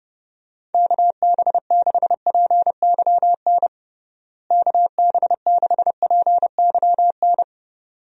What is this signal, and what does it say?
Morse code, heard as a keyed tone, about 31 wpm, 700 Hz. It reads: KB6PYN KB6PYN
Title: USA Call Signs - 6 Characters - Repeated 2x - 30wpm